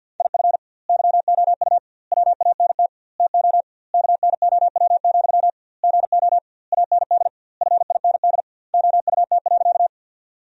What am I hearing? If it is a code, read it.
Morse code, heard as a keyed tone, about 47 wpm, 700 Hz. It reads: IF YOU WANT TO KNOW, GO AND FIND OUT.
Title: 2x Word-Spacing - Sentences from Top 200 Words - 50wpm